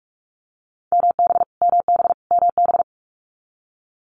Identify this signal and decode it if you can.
Morse code, heard as a keyed tone, about 46 wpm, 700 Hz. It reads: MB MB MB
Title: Canadian Provinces and Territories - Repeated 3x - 45wpm